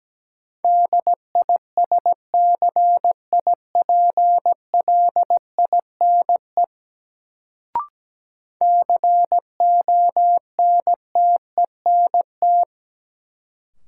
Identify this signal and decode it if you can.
Morse code, heard as a keyed tone, about 17 wpm, 700 Hz. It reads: DISCIPLINE CONTENT
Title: Sentences from Top 900 Words - New Word Drill - 17wpm